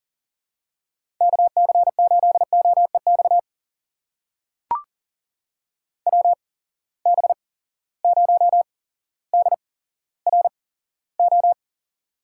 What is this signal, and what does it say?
Morse code, heard as a keyed tone, about 42 wpm, spaced as for 17 wpm, 700 Hz. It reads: KC8OEX WB0DRO
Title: USA Call Signs - 6 Characters - 8x Character Spacing - 40wpm